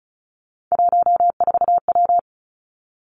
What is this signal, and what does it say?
Morse code, heard as a keyed tone, about 36 wpm, 700 Hz. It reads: 14W
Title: Three Challenging Letter-Number Combinations 35wpm